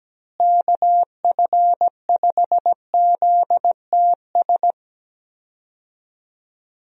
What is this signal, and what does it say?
Morse code, heard as a keyed tone, about 17 wpm, 700 Hz. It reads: KF5ZTS
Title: USA Call Signs - 6 Characters - Repeated 3x - 17wpm